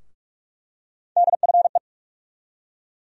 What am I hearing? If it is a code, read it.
Morse code, heard as a keyed tone, about 46 wpm, 700 Hz. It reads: DFE
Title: USA Call Sign Postfixes - 3 Letters 45wpm